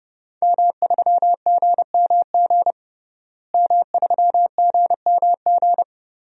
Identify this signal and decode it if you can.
Morse code, heard as a keyed tone, about 30 wpm, 700 Hz. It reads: M3GMG M3GMG
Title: DXCC-100 Call Signs - 5 Characters - Repeated 3x - 30wpm